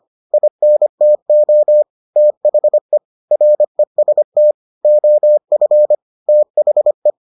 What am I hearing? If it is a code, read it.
Morse code, heard as a keyed tone, about 25 wpm, 600 Hz. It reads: INTO THE REST OF THE